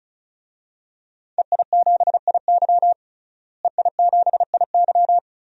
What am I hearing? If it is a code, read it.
Morse code, heard as a keyed tone, about 36 wpm, 700 Hz. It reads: EI7IY EI7IY
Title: DXCC-100 Call Signs - 5 Characters - Repeated 3x - 35wpm